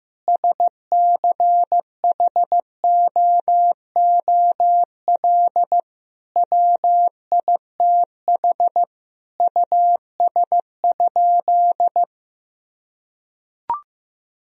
Morse code, heard as a keyed tone, about 15 wpm, 700 Hz. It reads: SCHOOL WITH US?
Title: Sentences from Top 1000 Words - Review Part 3 - 15wpm